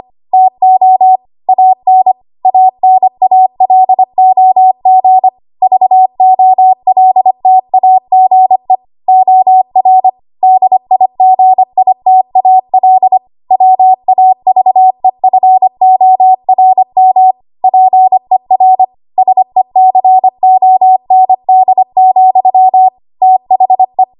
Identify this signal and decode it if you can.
Morse code, heard as a keyed tone, about 25 wpm, 750 Hz. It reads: TO AN ANALOG VOLTAGE OR DIGITAL WAVEFORM PER SECOND, THE
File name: Archive / 25 WPM / 25WPM